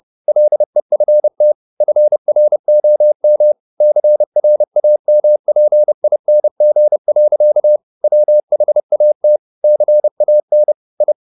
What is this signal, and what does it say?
Morse code, heard as a keyed tone, about 30 wpm, 600 Hz. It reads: LEFT FROM CRAMPING. WHAT CAN I